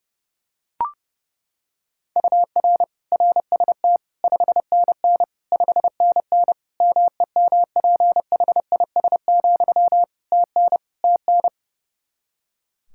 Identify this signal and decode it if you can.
Morse code, heard as a keyed tone, about 32 wpm, 700 Hz. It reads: UR RST 5NN 5NN MEMPHIS, TN TN
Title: Practice QSO - Exchange 1 - Beginning - 30wpm